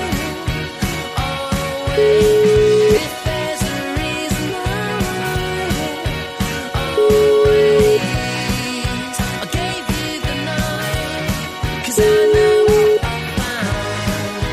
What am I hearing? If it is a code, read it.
Ringback tone (Europe, 425 Hz), with noise.